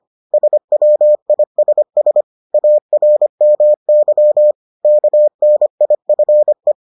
Morse code, heard as a keyed tone, about 25 wpm, 600 Hz. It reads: SWISS ARMY KNIFE